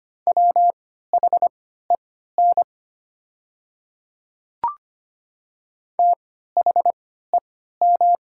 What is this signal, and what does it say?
Morse code, heard as a keyed tone, about 25 wpm, spaced as for 14 wpm, 700 Hz. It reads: WHEN THEM
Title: 3x Character Spacing - Top 100 Words - 25wpm